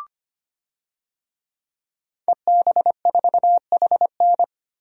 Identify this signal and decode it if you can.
Morse code, heard as a keyed tone, about 25 wpm, 700 Hz. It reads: EB4HN
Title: DXCC-100 Call Signs - 5 Characters - 25wpm